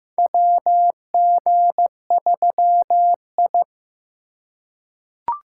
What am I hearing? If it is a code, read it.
Morse code, heard as a keyed tone, about 16 wpm, 700 Hz. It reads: WG3I
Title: USA Call Signs - 4 Characters - 2x Character Spacing - 15wpm